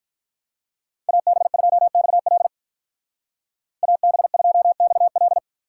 Morse code, heard as a keyed tone, about 47 wpm, 700 Hz. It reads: AB2XL AB2XL